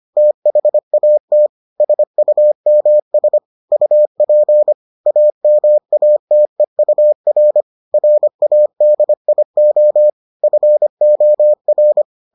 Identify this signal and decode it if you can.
Morse code, heard as a keyed tone, about 25 wpm, 600 Hz. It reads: THAT SUMS UP AMATEUR RADIO FOR